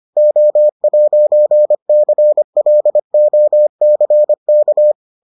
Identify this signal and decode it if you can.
Morse code, heard as a keyed tone, about 25 wpm, 600 Hz. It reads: O'CLOCK